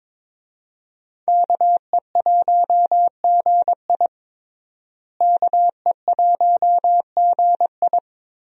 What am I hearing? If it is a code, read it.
Morse code, heard as a keyed tone, about 22 wpm, 700 Hz. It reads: KE1GI KE1GI
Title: USA Call Signs - 5 Characters - Repeated 2x - 22wpm